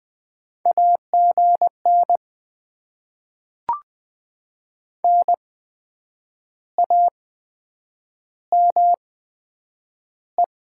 Morse code, heard as a keyed tone, about 20 wpm, spaced as for 8 wpm, 700 Hz. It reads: AGN NAME
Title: Top 20 QSO Elements - 8x Character Spacing - 20wpm